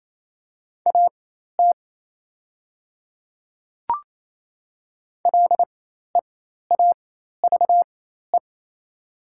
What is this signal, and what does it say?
Morse code, heard as a keyed tone, about 27 wpm, spaced as for 13 wpm, 700 Hz. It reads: AT LEAVE
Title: 4x Character Spacing - Top 500 Words - 28wpm